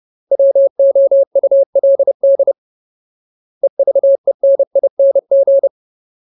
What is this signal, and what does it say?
Morse code, heard as a keyed tone, about 30 wpm, 550 Hz. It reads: WOULD EVENING
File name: Ultimate Word List / 30WPM Ultimate